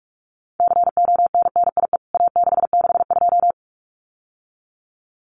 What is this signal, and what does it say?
Morse code, heard as a keyed tone, about 48 wpm, 700 Hz. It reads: CONNIE A662
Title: CWT Exchange 45wpm